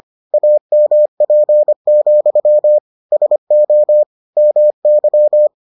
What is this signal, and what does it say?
Morse code, heard as a keyed tone, about 25 wpm, 600 Hz. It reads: AMP, SO MY